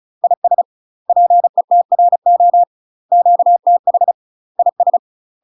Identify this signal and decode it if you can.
Morse code, heard as a keyed tone, about 36 wpm, 700 Hz. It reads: IS PETRO QTH IS
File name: UK MASTER QSOs TRAINING / 35WPM / UK MASTER QSOs 0028